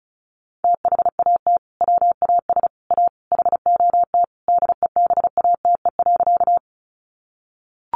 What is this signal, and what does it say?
Morse code, heard as a keyed tone, about 34 wpm, 700 Hz. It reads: THAT WAS A HOT DEBATE.
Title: Sentences from Top 1100 Words 35wpm